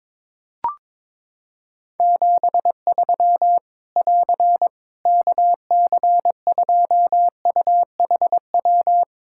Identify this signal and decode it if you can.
Morse code, heard as a keyed tone, about 22 wpm, 700 Hz. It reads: 73 <AR> KC2UHW
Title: Practice QSO - Exchange 4 - Ending - 22wpm